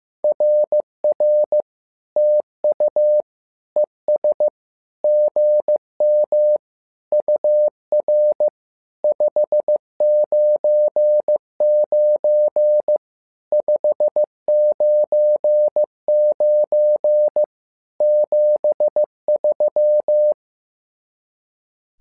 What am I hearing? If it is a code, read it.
Morse code, heard as a keyed tone, about 15 wpm, 600 Hz. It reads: RR TU ES GM UR 599 599 73
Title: SOTA QSO Practice 15wpm